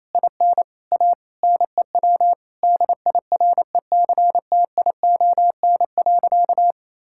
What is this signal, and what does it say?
Morse code, heard as a keyed tone, about 28 wpm, 700 Hz. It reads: IN A NEW DIRECTION.
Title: Sentences from Top 1000 Words - Review Part 5 - 28wpm